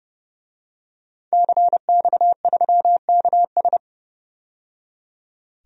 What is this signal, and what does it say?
Morse code, heard as a keyed tone, about 30 wpm, 700 Hz. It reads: CX3KS
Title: DXCC-100 Call Signs - 5 Characters - 30wpm